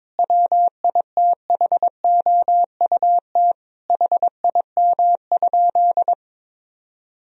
Morse code, heard as a keyed tone, about 21 wpm, 700 Hz. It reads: WITHOUT HIM?
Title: Sentences from Top 1000 Words - Review Part 3 - 22wpm